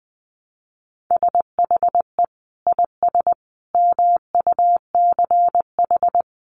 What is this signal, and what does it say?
Morse code, heard as a keyed tone, about 20 wpm, 700 Hz. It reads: SHE IS MUCH
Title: Sentences from Top 400 Words 20wpm